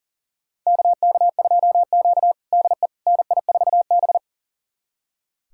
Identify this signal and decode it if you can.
Morse code, heard as a keyed tone, about 37 wpm, 700 Hz. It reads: KK2Q DE NI4B
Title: USA Callsign DE Callsign - 4 Characters - 40wpm